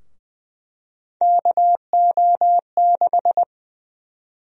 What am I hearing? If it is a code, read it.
Morse code, heard as a keyed tone, about 20 wpm, 700 Hz. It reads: KO6
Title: USA Call Sign Prefixes - 20wpm